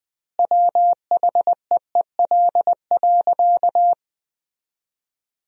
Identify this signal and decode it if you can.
Morse code, heard as a keyed tone, about 21 wpm, 700 Hz. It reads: WHEEL.
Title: 3x Word-Spacing - Sentences from Top 500 Words - 20wpm